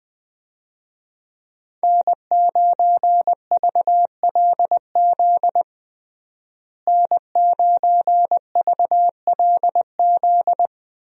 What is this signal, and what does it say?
Morse code, heard as a keyed tone, about 20 wpm, 700 Hz. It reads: N9VLZ N9VLZ
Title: USA Call Signs - 5 Characters - Repeated 2x - 20wpm